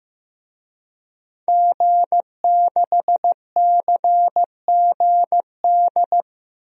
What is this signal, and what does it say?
Morse code, heard as a keyed tone, about 15 wpm, 700 Hz. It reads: G6CGD
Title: DXCC-100 Call Signs - 5 Characters - Repeated 2x - 15wpm